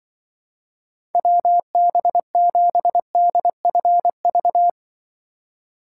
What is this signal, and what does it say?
Morse code, heard as a keyed tone, about 24 wpm, 700 Hz. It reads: WB7DFV
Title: Speed Racing - USA Call Signs 20wpm